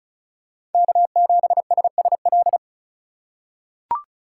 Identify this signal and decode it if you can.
Morse code, heard as a keyed tone, about 36 wpm, 700 Hz. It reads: K7SSL
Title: USA Call Signs - 5 Characters - 2x Character Spacing - 35wpm